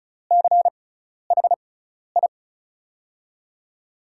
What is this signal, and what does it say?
Morse code, heard as a keyed tone, about 36 wpm, spaced as for 12 wpm, 690 Hz. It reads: CHI
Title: USA Call Signs - 6 Characters - 6x Character Spacing - 35wpm